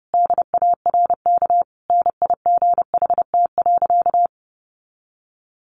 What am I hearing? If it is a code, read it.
Morse code, heard as a keyed tone, about 28 wpm, 700 Hz. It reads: DARK NIGHT.
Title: Sentences from Top 1300 Words 30wpm